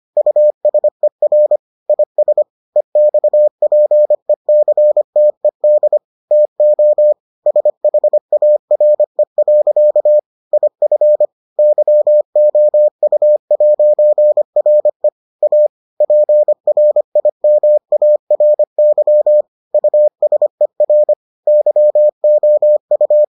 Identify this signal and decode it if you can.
Morse code, heard as a keyed tone, about 25 wpm, 600 Hz. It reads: USER IS EXPECTED TO SHARE. IF YOU'RE A PRIMARY USER YOU